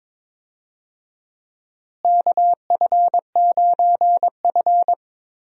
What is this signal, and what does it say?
Morse code, heard as a keyed tone, about 21 wpm, 700 Hz. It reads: KF9F
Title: USA Call Signs - 4 Characters - Repeated 2x - 22wpm